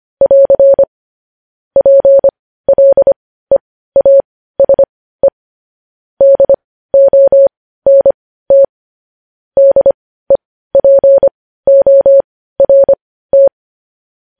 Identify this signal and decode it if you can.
Morse code, heard as a keyed tone, about 25 wpm, spaced as for 15 wpm, 550 Hz. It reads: <AR> PLEASE DONT DEPORT
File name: podcast / News headlines-1x-25x15